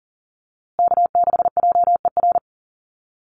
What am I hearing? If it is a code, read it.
Morse code, heard as a keyed tone, about 40 wpm, 700 Hz. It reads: K6JER
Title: USA Call Signs Repeated 3x 40wpm